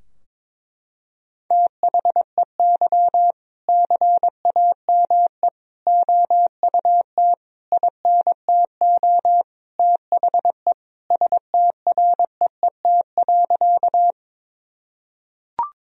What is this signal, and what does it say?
Morse code, heard as a keyed tone, about 22 wpm, 700 Hz. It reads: THEY CAME OUT INTO THE STREET.
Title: Sentences from Top 1000 Words - Review Part 3 - 22wpm